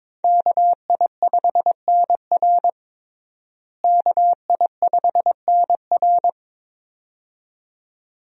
Morse code, heard as a keyed tone, about 22 wpm, 700 Hz. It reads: KI5NR KI5NR
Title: USA Call Signs - 5 Characters - Repeated 2x - 22wpm